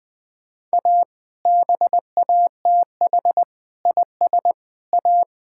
Morse code, heard as a keyed tone, about 21 wpm, 700 Hz. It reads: A BATH IS A
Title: Sentences from Top 1600 Words 20wpm